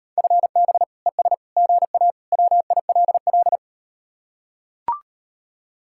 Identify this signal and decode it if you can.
Morse code, heard as a keyed tone, about 38 wpm, 700 Hz. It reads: FB ES GA WILL